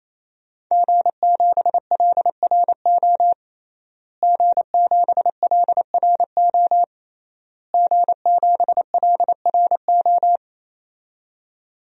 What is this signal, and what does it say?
Morse code, heard as a keyed tone, about 27 wpm, 700 Hz. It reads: G7LRO G7LRO G7LRO
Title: DXCC-100 Call Signs - 5 Characters - Repeated 3x - 28wpm